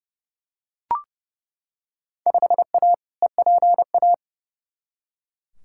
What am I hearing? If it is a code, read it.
Morse code, heard as a keyed tone, about 30 wpm, 700 Hz. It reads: 5A EPA
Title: Field Day Exchanges - 30wpm